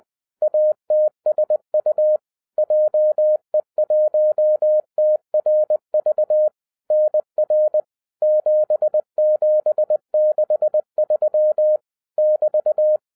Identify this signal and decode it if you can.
Morse code, heard as a keyed tone, about 20 wpm, 610 Hz. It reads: ATSU JE1TRV NR 7763 <BT>